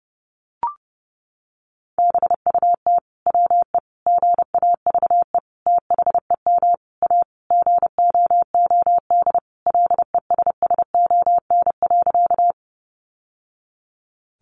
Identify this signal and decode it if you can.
Morse code, heard as a keyed tone, about 30 wpm, 700 Hz. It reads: BUT WE GAVE THEM A GOOD LESSON.
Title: Sentences from Top 1200 Words 30wpm